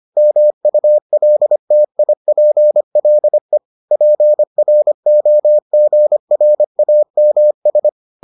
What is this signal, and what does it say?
Morse code, heard as a keyed tone, about 25 wpm, 600 Hz. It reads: MULTIPLE PROGRAMS